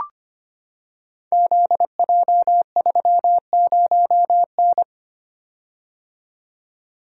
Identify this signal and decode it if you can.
Morse code, heard as a keyed tone, about 26 wpm, 700 Hz. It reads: ZJ30N
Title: Code Groups - 5 Letter:Numbers - 25wpm